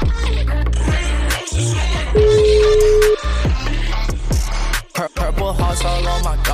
Ringback tone (Europe, 425 Hz), with noise.